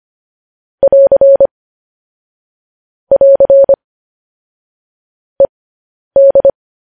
Morse code, heard as a keyed tone, about 25 wpm, spaced as for 10 wpm, 550 Hz. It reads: <AR> <AR> ED
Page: news headlines in morse code at 10 wpm